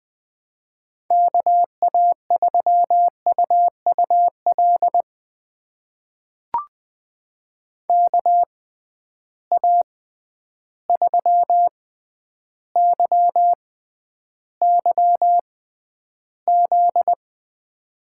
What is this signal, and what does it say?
Morse code, heard as a keyed tone, about 20 wpm, spaced as for 10 wpm, 700 Hz. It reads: KA3UUL KA3YYZ
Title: USA Call Signs - 6 Characters - 6x Character Spacing - 20wpm